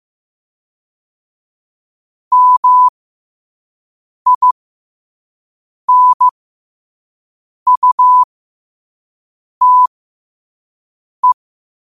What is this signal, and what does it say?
Morse code, heard as a keyed tone, about 15 wpm, spaced as for 5 wpm, 1000 Hz. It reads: MINUTE